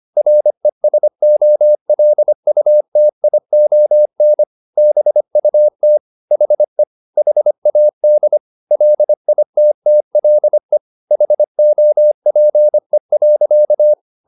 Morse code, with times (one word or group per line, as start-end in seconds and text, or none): 0.2-4.4 RESOLUTION
4.8-6.0 BUT
6.3-6.8 HE
7.2-8.4 HAD
8.7-10.8 LITTLE
11.1-13.9 HOPE.